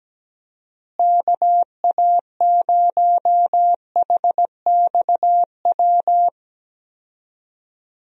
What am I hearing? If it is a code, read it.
Morse code, heard as a keyed tone, about 17 wpm, 700 Hz. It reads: KA0HXW
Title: USA Call Signs - 6 Characters - Repeated 2x - 17wpm